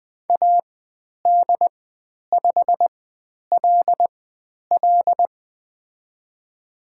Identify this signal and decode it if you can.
Morse code, heard as a keyed tone, about 20 wpm, spaced as for 10 wpm, 700 Hz. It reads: AD5LL
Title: 102 Lesson 17g - Letter u - Call Signs 20wpm